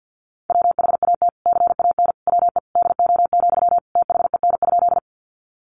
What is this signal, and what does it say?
Morse code, heard as a keyed tone, about 50 wpm, 700 Hz. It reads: WHAT CAN WE DO, THEN?
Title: Sentences from Top 100 Words 50wpm